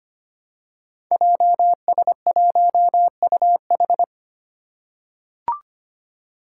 Morse code, heard as a keyed tone, about 25 wpm, 700 Hz. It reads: JS1UH
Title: Five Challenging Letter-Number Combinations 25wpm